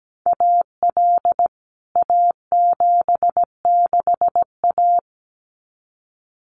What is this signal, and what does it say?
Morse code, heard as a keyed tone, about 17 wpm, 700 Hz. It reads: AL A76A
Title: CWT Exchange 17wpm